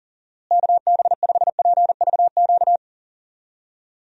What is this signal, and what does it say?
Morse code, heard as a keyed tone, about 40 wpm, 700 Hz. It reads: KB5PVQ
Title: USA Call Signs - 6 Characters - Repeated 3x - 40wpm